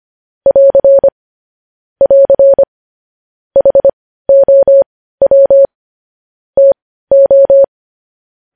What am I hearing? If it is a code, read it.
Morse code, heard as a keyed tone, about 25 wpm, spaced as for 15 wpm, 550 Hz. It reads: <AR> <AR> HOW TO
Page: news headlines in morse code at 15 wpm